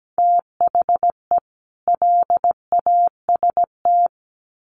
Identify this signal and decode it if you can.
Morse code, heard as a keyed tone, about 17 wpm, 700 Hz. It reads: THE LAST
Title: Sets of 2 Words 17wpm